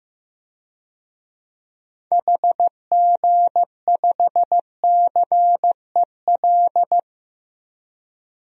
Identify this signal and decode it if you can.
Morse code, heard as a keyed tone, about 15 wpm, 700 Hz. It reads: HG5CEL
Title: DXCC-100 Call Signs - 6 Characters - 15wpm